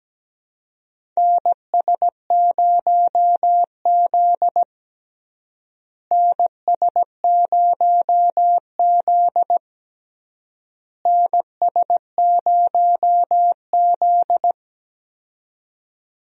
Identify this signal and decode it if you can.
Morse code, heard as a keyed tone, about 17 wpm, 700 Hz. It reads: NS0Z NS0Z NS0Z
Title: USA Call Signs - 4 Characters - Repeated 3x - 17wpm